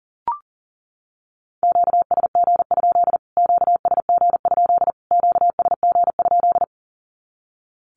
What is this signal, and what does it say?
Morse code, heard as a keyed tone, about 40 wpm, 700 Hz. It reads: QSG? QSG? QSG?
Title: Q-Codes Repeated 3x 40wpm